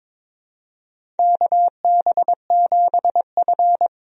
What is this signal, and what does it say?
Morse code, heard as a keyed tone, about 23 wpm, 700 Hz. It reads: KB7F
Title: USA Call Signs - 4 Characters - 22wpm